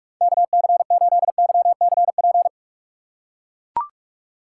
Morse code, heard as a keyed tone, about 42 wpm, 700 Hz. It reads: KC8YCP
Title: USA Call Signs - 6 Characters - 4x Character Spacing - 45wpm